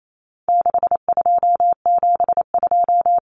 Morse code, heard as a keyed tone, about 28 wpm, 700 Hz. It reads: 6272
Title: Common Cut Numbers - 4-Digit 28wpm